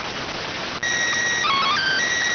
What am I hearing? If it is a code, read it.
SSTV (slow-scan television), PD 290.